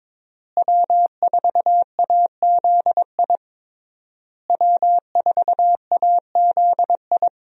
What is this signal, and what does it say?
Morse code, heard as a keyed tone, about 21 wpm, 700 Hz. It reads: W4AZI W4AZI
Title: USA Call Signs - 5 Characters - Repeated 3x - 22wpm